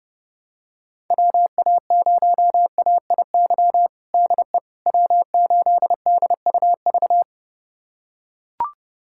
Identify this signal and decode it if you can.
Morse code, heard as a keyed tone, about 30 wpm, 700 Hz. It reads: WA0AIY DE W8DUV
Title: USA Callsign DE Callsign 30wpm